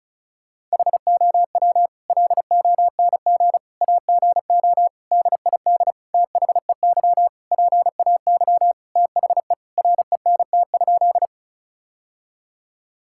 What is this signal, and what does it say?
Morse code, heard as a keyed tone, about 34 wpm, 700 Hz. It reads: HOW LONG AGO DID THEY PAY THE RENT?
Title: Sentences from Top 1000 Words - Review Part 5 - 35wpm